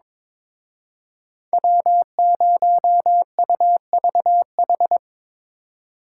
Morse code, heard as a keyed tone, about 26 wpm, 700 Hz. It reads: W0UVH
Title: Speed Racing - USA 1x3 Call Signs 22wpm